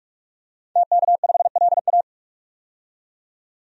Morse code, heard as a keyed tone, about 46 wpm, 700 Hz. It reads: TK5LA
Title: DXCC-100 Call Signs - 5 Characters - 45wpm